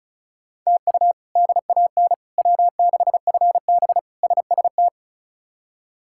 Morse code, heard as a keyed tone, about 36 wpm, 700 Hz. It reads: TU DAN W6FB SST
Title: K1USN SST QSO Practice - 35wpm